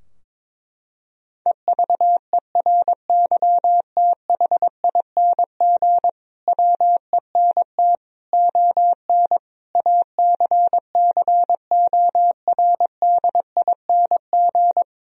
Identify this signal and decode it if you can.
Morse code, heard as a keyed tone, about 22 wpm, 700 Hz. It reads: EVERYTHING WENT ON ACCORDING
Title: Sentences from Top 1000 Words - Review Part 4 - 22wpm